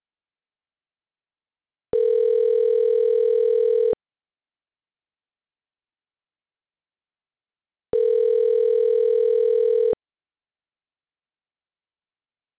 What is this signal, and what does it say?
Ringback tone (North America), 440 + 480 Hz.